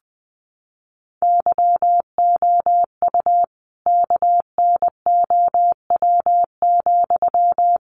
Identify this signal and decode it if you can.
Morse code, heard as a keyed tone, about 20 wpm, 700 Hz. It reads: YOU KNOW,
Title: Sentences from Top 200 Words 20wpm